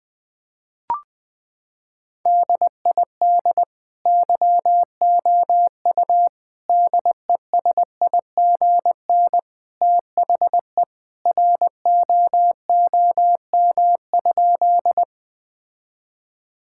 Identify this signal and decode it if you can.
Morse code, heard as a keyed tone, about 20 wpm, 700 Hz. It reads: DID YOU DESIGN THE ROOM?
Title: Sentences from Top 1000 Words - Review Part 5 - 20wpm